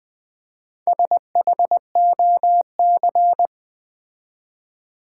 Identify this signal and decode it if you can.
Morse code, heard as a keyed tone, about 21 wpm, 700 Hz. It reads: SHOC
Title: Build A Word - 4 Letters At A Time - 8-Letter Words - 20wpm